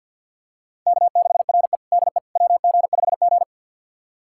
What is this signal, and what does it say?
Morse code, heard as a keyed tone, about 49 wpm, 700 Hz. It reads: K6RE DE WG5G
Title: USA Callsign DE Callsign - 4 Characters - 50wpm